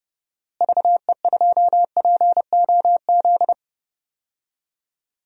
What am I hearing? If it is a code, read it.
Morse code, heard as a keyed tone, about 30 wpm, 700 Hz. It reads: VE2POZ
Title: DXCC-100 Call Signs - 6 Characters - Repeated 2x - 30wpm